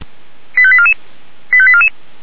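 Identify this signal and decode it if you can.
Five-tone selective call (ZVEI), heard as quick runs of five tones, tones of about 70 ms. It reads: 85844 85844
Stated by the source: ZVEI five-tone call 85371, received on SDR.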